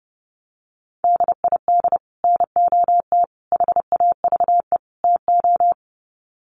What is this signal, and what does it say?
Morse code, heard as a keyed tone, about 30 wpm, 700 Hz. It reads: DID NOT HAVE TO
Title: Sets of 4 Words 30wpm